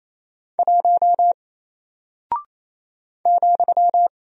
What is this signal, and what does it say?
Morse code, heard as a keyed tone, about 28 wpm, 700 Hz. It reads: 1 ,
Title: Single Character (Letter:Number:Punctuation:Prosign) 28wpm